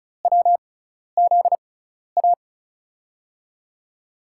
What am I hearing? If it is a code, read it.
Morse code, heard as a keyed tone, about 36 wpm, spaced as for 12 wpm, 700 Hz. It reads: WZA
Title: USA Call Signs - 6 Characters - 6x Character Spacing - 35wpm